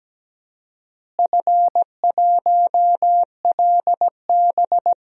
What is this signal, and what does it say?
Morse code, heard as a keyed tone, about 17 wpm, 700 Hz. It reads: F1LB
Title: DXCC-100 Call Signs - 4 Characters - Repeated 2x - 17wpm